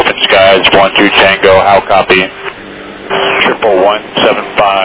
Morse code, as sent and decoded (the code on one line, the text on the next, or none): none